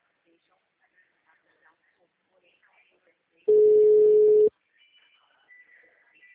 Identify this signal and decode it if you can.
Ringback tone (Europe, 425 Hz), with speech.